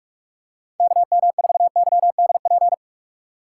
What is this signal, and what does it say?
Morse code, heard as a keyed tone, about 46 wpm, 700 Hz. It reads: KM4YDP
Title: USA Call Signs - 2x Character Spacing - 45wpm